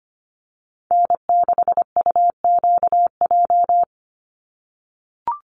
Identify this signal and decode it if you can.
Morse code, heard as a keyed tone, about 25 wpm, 700 Hz. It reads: N6UQJ
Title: USA Call Signs 25wpm